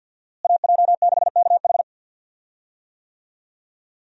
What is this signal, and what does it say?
Morse code, heard as a keyed tone, about 47 wpm, 700 Hz. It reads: AJ6KH
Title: USA Call Signs - 5 Characters - Repeated 3x - 50wpm